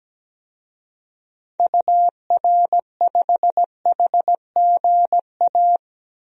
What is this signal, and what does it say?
Morse code, heard as a keyed tone, about 17 wpm, 700 Hz. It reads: UR5HGA
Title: DXCC-100 Call Signs - 6 Characters - 17wpm